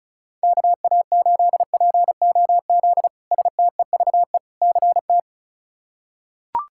Morse code, heard as a keyed tone, about 34 wpm, 700 Hz. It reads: KA8POZ STEVE CT
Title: K1USN SST QSO Practice - 35wpm